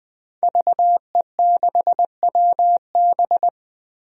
Morse code, heard as a keyed tone, about 20 wpm, 700 Hz. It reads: VE6WB
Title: DXCC-100 Call Signs - 5 Characters - 20wpm